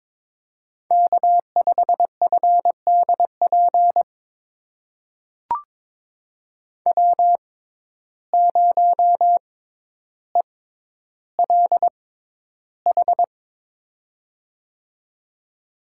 Morse code, heard as a keyed tone, about 21 wpm, spaced as for 11 wpm, 700 Hz. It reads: K5FDP W0ELH
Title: USA Call Signs - 6x Character Spacing - 22wpm